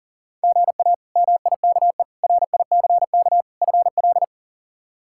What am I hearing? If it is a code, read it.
Morse code, heard as a keyed tone, about 40 wpm, 700 Hz. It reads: GA MIKE RICK FL
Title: K1USN SST QSO Practice - 40wpm